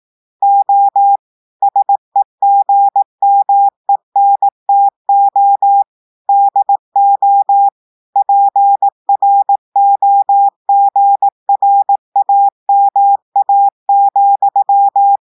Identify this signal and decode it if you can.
Morse code, heard as a keyed tone, about 18 wpm, 800 Hz. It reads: O SEGMENTO DO PROGRAMA,
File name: portugais texte en clair 2 vitesse 18 mots minutes